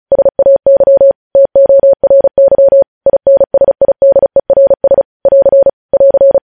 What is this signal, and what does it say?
Morse code, heard as a keyed tone, about 36 wpm, 550 Hz. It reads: SAY TORY INSIDERS <AR> <AR>
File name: podcast / News headlines-1x-35x35